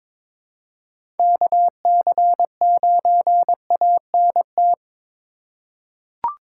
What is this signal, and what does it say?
Morse code, heard as a keyed tone, about 22 wpm, 700 Hz. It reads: KC9ANT
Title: USA Call Signs - 6 Characters - 4x Character Spacing - 22wpm